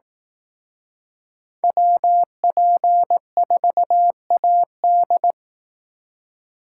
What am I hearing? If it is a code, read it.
Morse code, heard as a keyed tone, about 18 wpm, 700 Hz. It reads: WP4AD
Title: Speed Racing - USA 2x2 Call Signs 15wpm